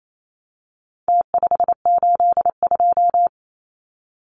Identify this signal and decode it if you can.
Morse code, heard as a keyed tone, about 28 wpm, 700 Hz. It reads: T582
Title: Numbers - Contest Style - 4-Digit 28wpm